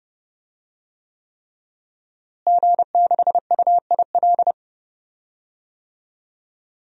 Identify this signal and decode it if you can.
Morse code, heard as a keyed tone, about 32 wpm, 700 Hz. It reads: G6UIL
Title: DXCC-100 Call Signs - 5 Characters - Repeated 3x - 30wpm